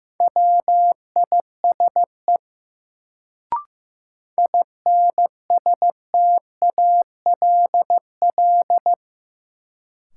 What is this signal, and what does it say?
Morse code, heard as a keyed tone, about 15 wpm, 700 Hz. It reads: WISE INSTALL
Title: Sentences from Top 1600 Words - New Word Drill Alt - 15wpm